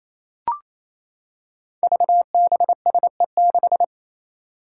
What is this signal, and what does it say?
Morse code, heard as a keyed tone, about 28 wpm, 700 Hz. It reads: VBSE6
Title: Five Challenging Letter-Number Combinations 28wpm